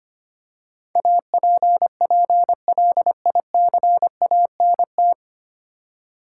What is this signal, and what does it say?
Morse code, heard as a keyed tone, about 25 wpm, 700 Hz. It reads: APPLICANT
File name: Nine Letter Words - 25wpm